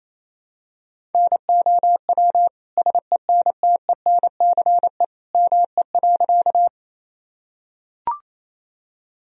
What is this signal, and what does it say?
Morse code, heard as a keyed tone, about 28 wpm, 700 Hz. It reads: NOW SENTENCE ME.
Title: Sentences from Top 1000 Words - Review Part 1 - 28wpm